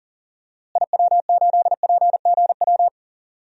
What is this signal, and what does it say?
Morse code, heard as a keyed tone, about 37 wpm, 700 Hz. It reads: IW8PGW
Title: DXCC-100 Call Signs - 6 Characters - Repeated 2x - 40wpm